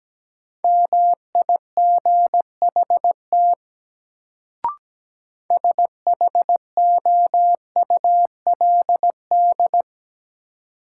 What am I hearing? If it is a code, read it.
Morse code, heard as a keyed tone, about 17 wpm, 700 Hz. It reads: MIGHT SHOULD
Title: Sentences from Top 300 Words - New Word Drill - 17wpm